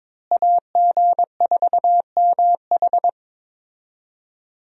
Morse code, heard as a keyed tone, about 22 wpm, 700 Hz. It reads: AG4MH
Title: USA Call Signs - 5 Characters - Repeated 3x - 22wpm